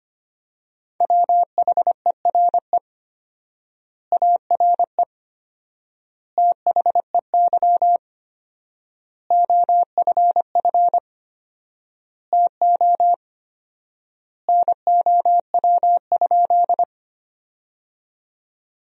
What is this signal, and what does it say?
Morse code, heard as a keyed tone, about 25 wpm, 700 Hz. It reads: WHERE ARE THEY OFF TO NOW?
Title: 4x Word-Spacing - Sentences from Top 200 Words - 25wpm